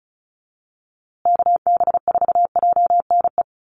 Morse code, heard as a keyed tone, about 34 wpm, 700 Hz. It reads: KB4JNE
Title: USA Call Signs Repeated 3x 35wpm